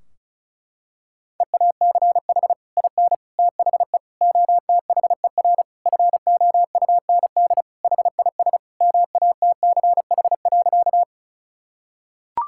Morse code, heard as a keyed tone, about 34 wpm, 700 Hz. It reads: EACH IN THE OTHER FOUND HIS MATCH.
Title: Sentences from Top 1600 Words 35wpm